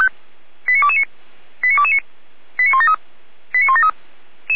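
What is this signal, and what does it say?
Five-tone selective call (ZVEI), heard as quick runs of five tones, tones of about 70 ms. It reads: 89229 89229 89173 89173
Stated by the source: ZVEI five-tone call received by SDR at feuerwehr-orlamuende.org, fire brigade Orlamünde.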